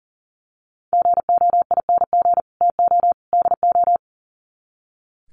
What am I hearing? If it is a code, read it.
Morse code, heard as a keyed tone, about 40 wpm, 700 Hz. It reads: GOING TO DO
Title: Sets of 3 Words 40wpm